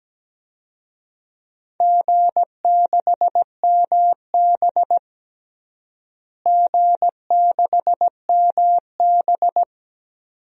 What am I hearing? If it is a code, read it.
Morse code, heard as a keyed tone, about 17 wpm, 700 Hz. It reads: G6MB G6MB
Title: DXCC-100 Call Signs - 4 Characters - Repeated 2x - 17wpm